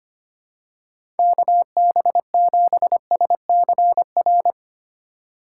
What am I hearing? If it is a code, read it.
Morse code, heard as a keyed tone, about 25 wpm, 700 Hz. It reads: KB7SCR
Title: USA Call Signs - 3x Character Spacing - 25wpm